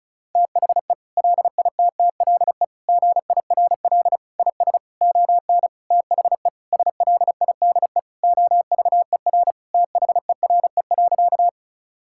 Morse code, heard as a keyed tone, about 36 wpm, 700 Hz. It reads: THE LITTLE GIRL IS ON THE SLIDE OVER THERE.
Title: Sentences from Top 1600 Words 35wpm